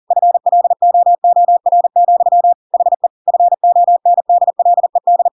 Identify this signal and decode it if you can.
Morse code, heard as a keyed tone, about 40 wpm, 700 Hz. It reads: FLOOR, HE FONDLED
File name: Pinocchio / Pinocchio ch7 40 40 ik1hge